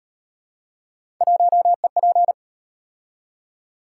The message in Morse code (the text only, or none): 1EP